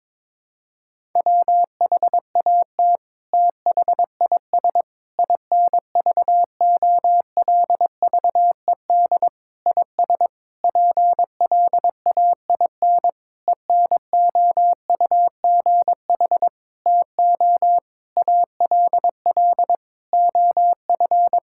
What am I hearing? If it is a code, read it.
Morse code, heard as a keyed tone, about 22 wpm, 700 Hz. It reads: WHAT THIS INVOLVED IS PLAIN ENOUGH TO ALL OF
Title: Sentences from Top 1000 Words - Review Part 5 - 22wpm